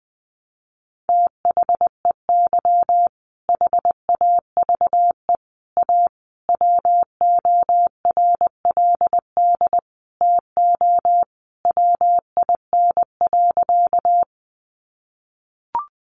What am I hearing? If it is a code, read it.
Morse code, heard as a keyed tone, about 20 wpm, 700 Hz. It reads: THEY HAVE A WORLD TO WIN.
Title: Sentences from Top 1500 Words 20wpm